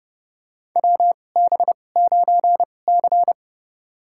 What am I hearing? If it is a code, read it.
Morse code, heard as a keyed tone, about 30 wpm, spaced as for 22 wpm, 700 Hz. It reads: WB9C
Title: USA Call Signs - 4 Characters - 2x Character Spacing - 30wpm